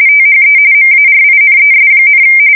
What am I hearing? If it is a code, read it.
RTTY, about 45.45 baud. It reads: THE MARINES,
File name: fsk170 / USS-New-Jersey